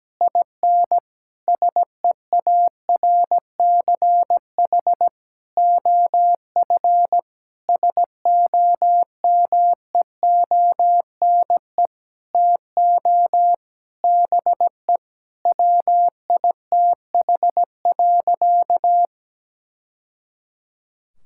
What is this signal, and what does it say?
Morse code, heard as a keyed tone, about 17 wpm, 700 Hz. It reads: IN SEARCH OF SOMEONE TO BE WITH.
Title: Sentences from Top 1000 Words - Review Part 5 - 17wpm